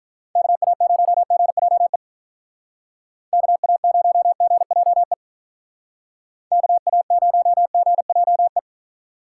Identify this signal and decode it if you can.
Morse code, heard as a keyed tone, about 43 wpm, 700 Hz. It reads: KA0GJE KA0GJE KA0GJE
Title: Speed Racing - USA Call Signs 35wpm